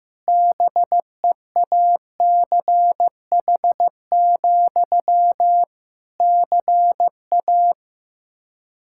Morse code, heard as a keyed tone, about 15 wpm, 700 Hz. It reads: BEACH, CA
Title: Practice QSO - Exchange 1 - Beginning - 15wpm